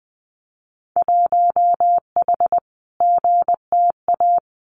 Morse code, heard as a keyed tone, about 20 wpm, 700 Hz. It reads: 1H GTA
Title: Winter Field Day Exchanges 20wpm